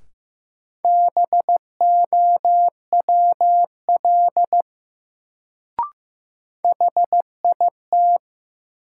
Morse code, heard as a keyed tone, about 15 wpm, 700 Hz. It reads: BOWL HIT
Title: Sentences from Top 1200 Words - New Word Drill Alt - 15wpm